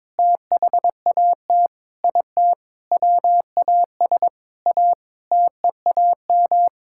Morse code, heard as a keyed tone, about 22 wpm, 700 Hz. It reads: THAT IT WAS A TEAM
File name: practice / 22wpm / Sentences with 8-Word Phrases - 22wpm